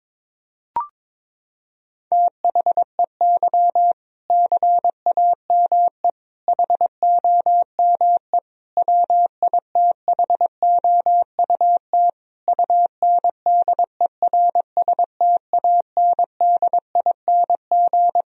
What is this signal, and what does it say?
Morse code, heard as a keyed tone, about 22 wpm, 700 Hz. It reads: THEY CAME HOME WITHOUT UNDERSTANDING
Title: Sentences from Top 1000 Words - Review Part 3 - 22wpm